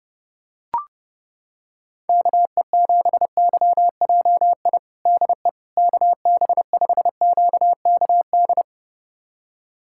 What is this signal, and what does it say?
Morse code, heard as a keyed tone, about 31 wpm, 700 Hz. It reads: KE7YJI DE KB5QKD
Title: USA Callsign DE Callsign - 6 Characters - 30wpm